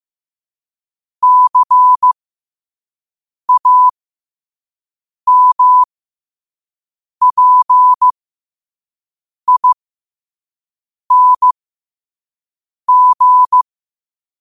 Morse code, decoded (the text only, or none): CAMPING